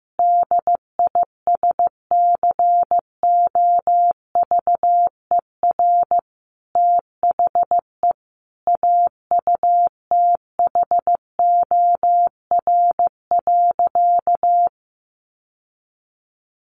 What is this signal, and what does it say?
Morse code, heard as a keyed tone, about 14 wpm, 700 Hz. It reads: DISCOVER THE AUTHOR.
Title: Sentences from Top 1200 Words 15wpm